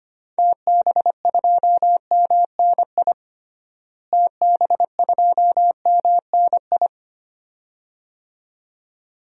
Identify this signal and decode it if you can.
Morse code, heard as a keyed tone, about 25 wpm, 700 Hz. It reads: TB2MNI TB2MNI
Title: DXCC-100 Call Signs - 6 Characters - Repeated 2x - 25wpm